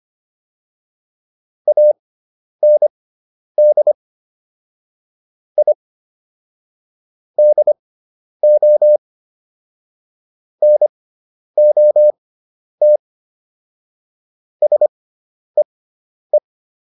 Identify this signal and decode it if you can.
Morse code, heard as a keyed tone, about 25 wpm, spaced as for 10 wpm, 600 Hz. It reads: AND I DO NOT SEE